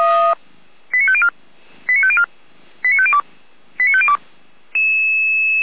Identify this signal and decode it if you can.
Five-tone selective call (ZVEI), heard as quick runs of five tones, tones of about 70 ms. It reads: 89504 89504 89502 89502